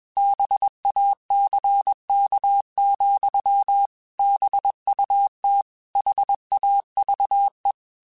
Morse code, heard as a keyed tone, about 21 wpm, 800 Hz. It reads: BACK, BUT HAVE
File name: QSOs / QSOs-mp3s / kc7haa@20